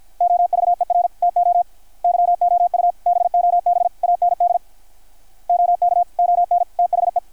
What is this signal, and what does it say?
Morse code, heard as a keyed tone, about 53 wpm, 660 Hz, with radio noise. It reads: OVER TO YOU BOB AND OK ON THE
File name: archived / CRSnet